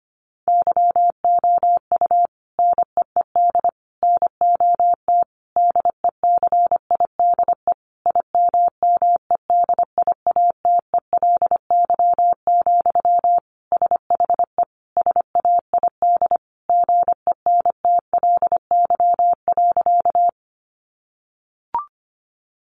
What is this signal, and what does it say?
Morse code, heard as a keyed tone, about 25 wpm, 700 Hz. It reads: YOU NEED NOT DECIDE IMMEDIATELY, SHE SAID GENTLY.
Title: Sentences from Top 1500 Words 25wpm